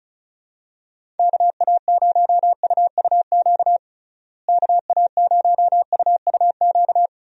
Morse code, heard as a keyed tone, about 34 wpm, 700 Hz. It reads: KA0UUQ KA0UUQ
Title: USA Call Signs - 6 Characters - Repeated 3x - 35wpm